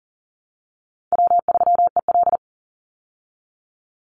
Morse code, heard as a keyed tone, about 40 wpm, 700 Hz. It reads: W3EL
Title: Call Signs Repeated 2x - 40wpm